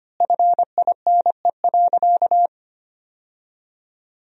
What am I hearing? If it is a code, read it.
Morse code, heard as a keyed tone, about 25 wpm, 700 Hz. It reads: FINE.
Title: 4x Word-Spacing - Sentences from Top 500 Words - 25wpm